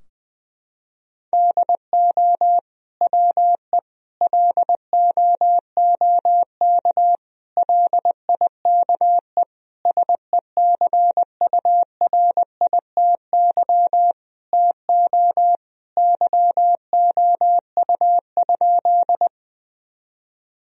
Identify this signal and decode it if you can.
Morse code, heard as a keyed tone, about 20 wpm, 700 Hz. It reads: DO WE LOOK LIKE SECURITY TO YOU?
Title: Sentences from Top 1000 Words - Review Part 4 - 20wpm